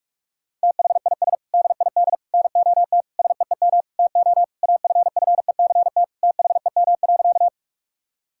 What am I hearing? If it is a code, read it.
Morse code, heard as a keyed tone, about 41 wpm, 700 Hz. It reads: THIS DID NOT SEEM TO AFFECT THEM.
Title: Sentences from Top 1000 Words - Review Part 6 - 45wpm